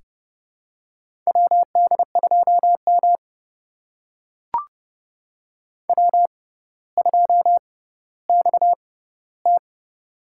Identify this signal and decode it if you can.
Morse code, heard as a keyed tone, about 30 wpm, spaced as for 15 wpm, 700 Hz. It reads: WD2M W2XT
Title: USA Call Signs - 4 Characters - 6x Character Spacing - 30wpm